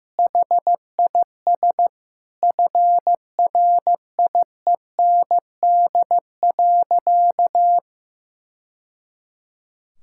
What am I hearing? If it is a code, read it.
Morse code, heard as a keyed tone, about 16 wpm, 700 Hz. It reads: HIS FRIEND.
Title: Sentences from Top 1000 Words - Review Part 5 - 15wpm